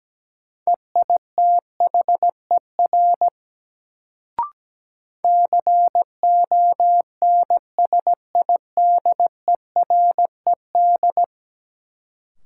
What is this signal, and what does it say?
Morse code, heard as a keyed tone, about 17 wpm, 700 Hz. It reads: EITHER CONSIDERED
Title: Sets of 5 Words - New Word Drill - 17wpm